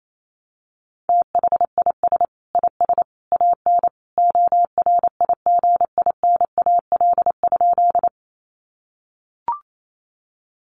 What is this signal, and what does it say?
Morse code, heard as a keyed tone, about 28 wpm, 700 Hz. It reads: THIS IS AN ORIGINAL?
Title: Sentences from Top 900 Words 28wpm